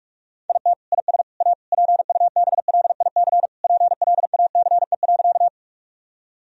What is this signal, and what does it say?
Morse code, heard as a keyed tone, about 47 wpm, 700 Hz. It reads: IT IS A PUBLIC PLACE.
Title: Sentences from Top 1000 Words - Review Part 3 - 45wpm